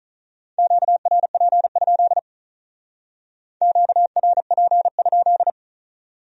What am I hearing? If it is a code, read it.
Morse code, heard as a keyed tone, about 45 wpm, 700 Hz. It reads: QRP? QRP?